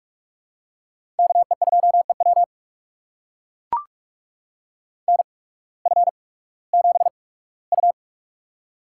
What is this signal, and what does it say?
Morse code, heard as a keyed tone, about 42 wpm, spaced as for 20 wpm, 700 Hz. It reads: KE2EW NF7U
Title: USA Call Signs - 8x Character Spacing - 45wpm